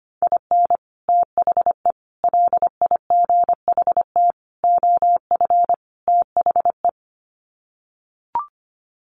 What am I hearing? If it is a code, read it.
Morse code, heard as a keyed tone, about 25 wpm, 700 Hz. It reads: IN THE LIGHT OF THE
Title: Sets of 5 Words Encore2 25wpm